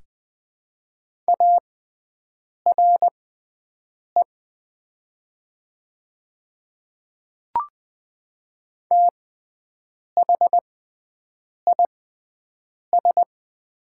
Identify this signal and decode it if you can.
Morse code, heard as a keyed tone, about 20 wpm, spaced as for 7 wpm, 700 Hz. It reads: ARE THIS
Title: 6x Character Spacing - Top 300 Words - 20wpm